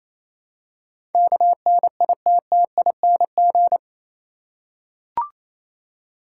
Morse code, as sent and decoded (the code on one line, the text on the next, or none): -.- -. .. - - .. -. --.
KNITTING